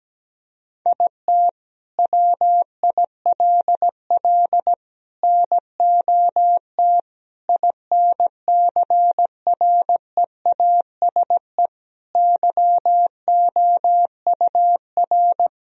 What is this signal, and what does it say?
Morse code, heard as a keyed tone, about 17 wpm, 700 Hz. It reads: IT WILL NOT INCREASE YOUR
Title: Sentences from Top 1000 Words - Review Part 4 - 17wpm